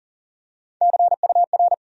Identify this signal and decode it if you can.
Morse code, heard as a keyed tone, about 40 wpm, 700 Hz. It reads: CUR
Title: USA Call Sign Postfixes - 3 Letters 40wpm